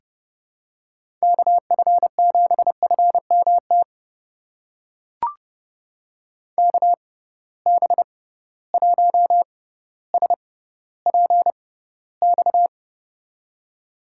Morse code, heard as a keyed tone, about 30 wpm, spaced as for 15 wpm, 700 Hz. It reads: KF7FMT KB1SPX
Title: USA Call Signs - 6 Characters - 6x Character Spacing - 30wpm